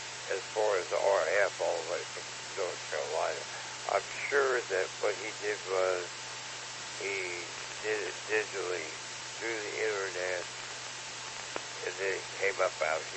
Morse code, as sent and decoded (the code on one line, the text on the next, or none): none